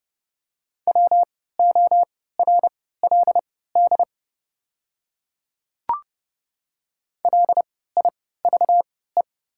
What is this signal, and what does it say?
Morse code, heard as a keyed tone, about 30 wpm, spaced as for 17 wpm, 700 Hz. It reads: WORLD LIVE
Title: 3x Character Spacing - Top 500 Words - 30wpm